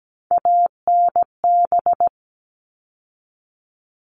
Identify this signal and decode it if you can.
Morse code, heard as a keyed tone, about 17 wpm, 700 Hz. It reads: ANB
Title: All Cut Numbers - 3-Digit 17wpm